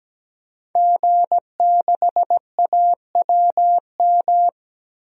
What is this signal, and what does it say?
Morse code, heard as a keyed tone, about 17 wpm, 700 Hz. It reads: G6AWM
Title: DXCC-100 Call Signs - 5 Characters - 17wpm